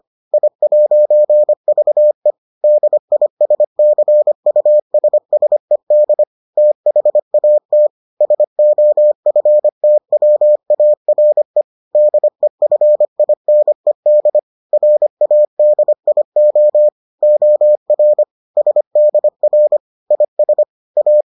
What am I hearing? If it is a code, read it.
Morse code, heard as a keyed tone, about 25 wpm, 600 Hz. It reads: I'VE DISCUSSED THAT SOFTWARE DEFINED RADIO OR SDR IS A